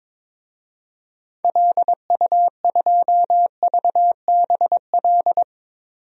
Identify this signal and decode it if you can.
Morse code, heard as a keyed tone, about 22 wpm, 700 Hz. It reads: LU2VBL
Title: DXCC-100 Call Signs - 6 Characters - 22wpm